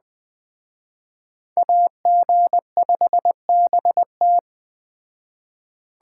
Morse code, heard as a keyed tone, about 19 wpm, 700 Hz. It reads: AG5BT